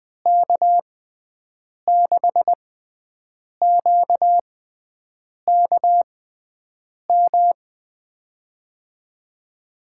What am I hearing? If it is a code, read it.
Morse code, heard as a keyed tone, about 20 wpm, spaced as for 7 wpm, 700 Hz. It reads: K6QKM
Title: USA Call Signs - 6 Characters - 6x Character Spacing - 20wpm